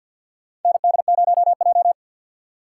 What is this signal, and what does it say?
Morse code, heard as a keyed tone, about 51 wpm, 700 Hz. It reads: ND0J
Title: USA Call Signs - 4 Characters - 3x Character Spacing - 50wpm